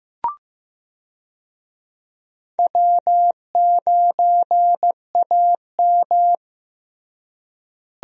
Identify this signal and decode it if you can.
Morse code, heard as a keyed tone, about 15 wpm, 700 Hz. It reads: W9AM
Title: USA Call Signs - 4 Characters - Repeated 3x - 15wpm